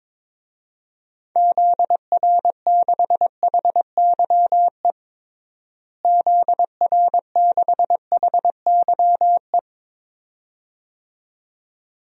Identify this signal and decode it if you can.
Morse code, heard as a keyed tone, about 22 wpm, 700 Hz. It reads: ZR6HYE ZR6HYE
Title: DXCC-100 Call Signs - 6 Characters - Repeated 2x - 22wpm